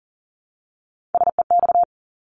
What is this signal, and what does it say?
Morse code, heard as a keyed tone, about 40 wpm, 700 Hz. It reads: SEX
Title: Top 1501-1750 Words 40wpm